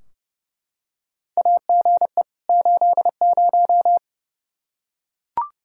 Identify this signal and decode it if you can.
Morse code, heard as a keyed tone, about 30 wpm, 700 Hz. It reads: AGE 80
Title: Practice QSO - Exchange 3 - More Info - 30wpm